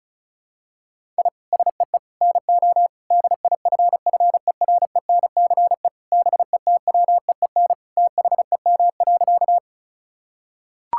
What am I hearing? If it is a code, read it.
Morse code, heard as a keyed tone, about 36 wpm, 700 Hz. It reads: I SEE NO DIFFERENCE BETWEEN THEM.
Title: Sentences from Top 1000 Words - Review Part 5 - 35wpm